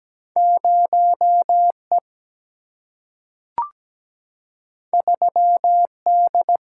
Morse code, heard as a keyed tone, about 17 wpm, 700 Hz. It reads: 0 E 3 D
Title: Single-Letter Single-Number 17wpm